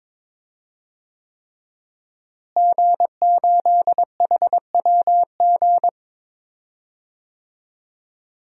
Morse code, heard as a keyed tone, about 22 wpm, 700 Hz. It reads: G8HWG
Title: DXCC-100 Call Signs - 5 Characters - Repeated 3x - 22wpm